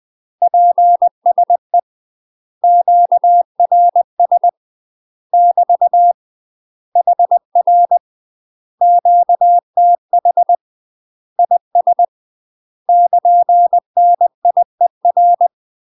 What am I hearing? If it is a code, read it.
Morse code, heard as a keyed tone, about 20 wpm, 700 Hz. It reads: PSE QRS <BT> HR QTH IS <KN> NIER
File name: UK MASTER QSOs TRAINING/20WPM